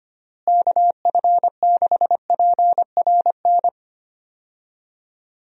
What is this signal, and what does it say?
Morse code, heard as a keyed tone, about 25 wpm, 700 Hz. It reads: KF6PRN
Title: USA Call Signs - 6 Characters - Repeated 2x - 25wpm